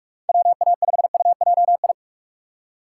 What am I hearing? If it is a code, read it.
Morse code, heard as a keyed tone, about 41 wpm, 700 Hz. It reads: WA5UJI
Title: USA Call Signs - 3x Character Spacing - 45wpm